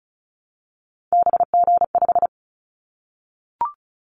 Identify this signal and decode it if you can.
Morse code, heard as a keyed tone, about 36 wpm, 700 Hz. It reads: BG5
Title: Callsign Prefixes for Top 100 DXCC - 35wpm